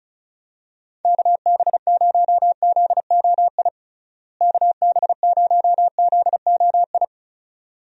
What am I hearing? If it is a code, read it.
Morse code, heard as a keyed tone, about 36 wpm, 700 Hz. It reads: KB0ZOI KB0ZOI
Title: USA Call Signs - 6 Characters - Repeated 2x - 35wpm